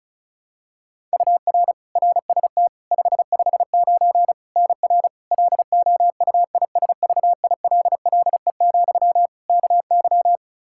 Morse code, heard as a keyed tone, about 36 wpm, 700 Hz. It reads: UR RST 559 NR LOUISVILLE, KY
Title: K1USN SST QSO Practice 35wpm